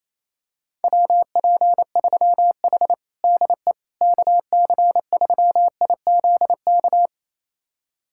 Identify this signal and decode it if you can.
Morse code, heard as a keyed tone, about 28 wpm, 700 Hz. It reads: WP3H DE KC3IZK
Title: USA Callsign DE Callsign 28wpm